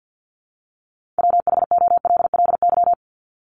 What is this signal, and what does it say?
Morse code, heard as a keyed tone, about 48 wpm, 700 Hz. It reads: WHOLLY
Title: LY Suffix - Top 100 Words with Base Words 50wpm